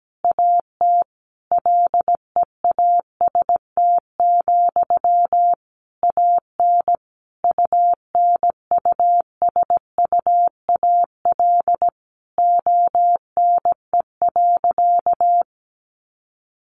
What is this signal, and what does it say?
Morse code, heard as a keyed tone, about 17 wpm, 700 Hz. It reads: AT LEAST, AN UNUSUAL ONE.
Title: Sentences from Top 1400 Words 17wpm